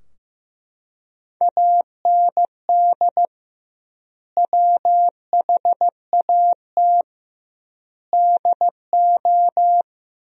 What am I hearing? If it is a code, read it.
Morse code, heard as a keyed tone, about 15 wpm, 700 Hz. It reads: AND WHAT DO
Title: 2x Word-Spacing - Sentences from Top 200 Words - 15wpm